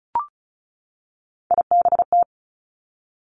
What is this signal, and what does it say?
Morse code, heard as a keyed tone, about 36 wpm, 700 Hz. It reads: IBT
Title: Three Challenging Letter-Number Combinations 35wpm